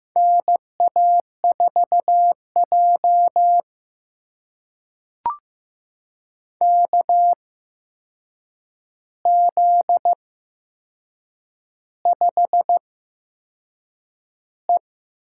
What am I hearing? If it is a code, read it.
Morse code, heard as a keyed tone, about 15 wpm, spaced as for 6 wpm, 700 Hz. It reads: NA4J KZ5E
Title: USA Call Signs - 4 Characters - 8x Character Spacing - 15wpm